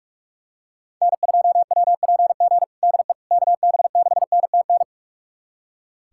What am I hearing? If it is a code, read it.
Morse code, heard as a keyed tone, about 47 wpm, 700 Hz. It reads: N2WPG DE KB6NTN
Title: USA Callsign DE Callsign 45wpm